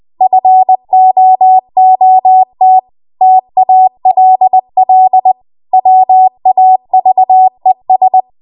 Morse code, heard as a keyed tone, about 20 wpm, 750 Hz. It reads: FOOT TALL WAVES
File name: Morse / 20WPM